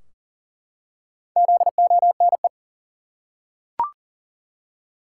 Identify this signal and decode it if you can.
Morse code, heard as a keyed tone, about 40 wpm, 700 Hz. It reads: ZONE
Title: Four Letter Words - 40wpm